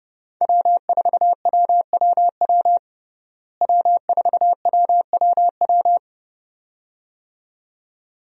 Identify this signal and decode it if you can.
Morse code, heard as a keyed tone, about 32 wpm, 700 Hz. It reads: W4WWW W4WWW
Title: USA Call Signs - 5 Characters - Repeated 2x - 30wpm